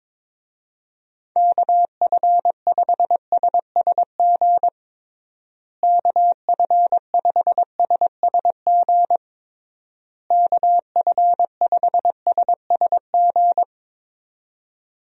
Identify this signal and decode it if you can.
Morse code, heard as a keyed tone, about 22 wpm, 700 Hz. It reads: KF5SSG KF5SSG KF5SSG
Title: USA Call Signs - 6 Characters - Repeated 3x - 22wpm